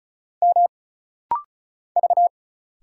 Morse code, heard as a keyed tone, about 34 wpm, 700 Hz. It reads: M V
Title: Single Letters - Rapid-Fire 35wpm